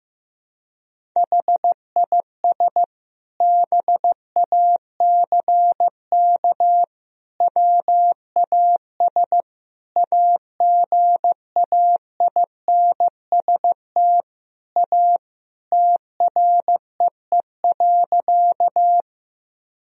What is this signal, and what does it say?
Morse code, heard as a keyed tone, about 15 wpm, 700 Hz. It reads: HIS BACK WAS AGAINST A TREE.
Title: Sentences from Top 1000 Words - Review Part 1 - 15wpm